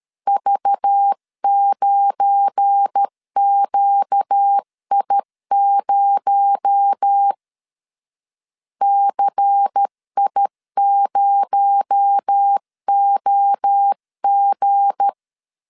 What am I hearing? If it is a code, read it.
Morse code, heard as a keyed tone, about 13 wpm, 800 Hz. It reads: V9QI0 CI0OG
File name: LEZIONI / 7 GF9Z / 7 GF9Z 12wpm 2147